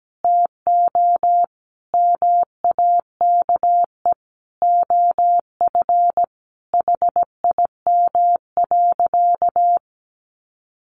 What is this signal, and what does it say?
Morse code, heard as a keyed tone, about 17 wpm, 700 Hz. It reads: TO MAKE OF HIM.
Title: Sentences from Top 1400 Words 17wpm